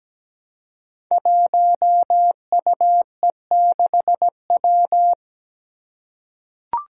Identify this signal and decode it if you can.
Morse code, heard as a keyed tone, about 17 wpm, 700 Hz. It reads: 1UE6W
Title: Five Challenging Letter-Number Combinations 17wpm